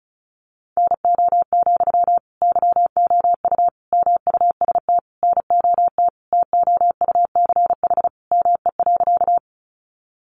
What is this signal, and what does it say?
Morse code, heard as a keyed tone, about 34 wpm, 700 Hz. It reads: NO, YOU MUST NOT TOUCH ME.
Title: Sentences from Top 1200 Words 35wpm